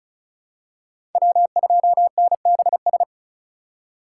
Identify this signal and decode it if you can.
Morse code, heard as a keyed tone, about 34 wpm, 700 Hz. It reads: W2NBS
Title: USA Call Signs - 5 Characters - 2x Character Spacing - 35wpm